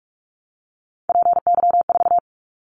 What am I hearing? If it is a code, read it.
Morse code, heard as a keyed tone, about 44 wpm, 700 Hz. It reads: PY4
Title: Callsign Prefixes for Top 50 DXCC - 45wpm